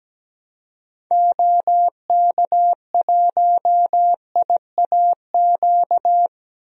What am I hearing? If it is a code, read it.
Morse code, heard as a keyed tone, about 17 wpm, 700 Hz. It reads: OK1IAQ
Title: DXCC-100 Call Signs - 6 Characters - 17wpm